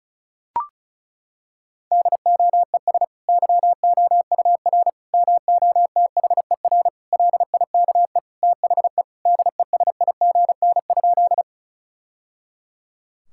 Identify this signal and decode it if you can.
Morse code, heard as a keyed tone, about 36 wpm, 700 Hz. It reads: DOES YOUR MOTHER LIKE THE DESIGN?
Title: Sentences from Top 1000 Words - Review Part 5 - 35wpm